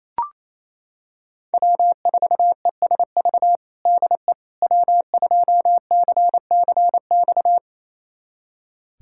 Morse code, heard as a keyed tone, about 28 wpm, 700 Hz. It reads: W4ESV DE W2CCX
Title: USA Callsign DE Callsign - 5 Characters - 28wpm